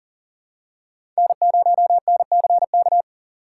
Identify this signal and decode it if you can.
Morse code, heard as a keyed tone, about 37 wpm, 700 Hz. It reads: N0NCK
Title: USA Call Signs - 5 Characters - 4x Character Spacing - 40wpm